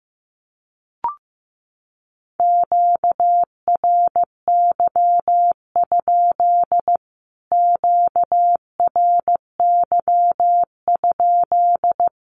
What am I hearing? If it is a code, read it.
Morse code, heard as a keyed tone, about 15 wpm, 700 Hz. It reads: QRY? QRY?
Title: Q-Codes Repeated 3x 15wpm